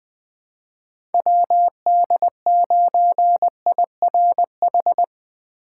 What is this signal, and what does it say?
Morse code, heard as a keyed tone, about 20 wpm, 700 Hz. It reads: WD9IRH
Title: USA Call Signs - 3x Character Spacing - 20wpm